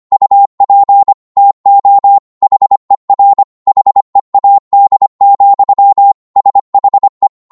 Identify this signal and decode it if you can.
Morse code, heard as a keyed tone, about 25 wpm, 800 Hz. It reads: UP TO HER HEAD, SHE
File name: Morse / psg-25wpm